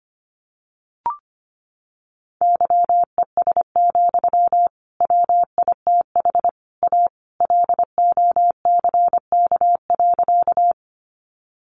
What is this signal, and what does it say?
Morse code, heard as a keyed tone, about 25 wpm, 700 Hz. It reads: YES, WITH A LOCK.
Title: Sentences from Top 1300 Words 25wpm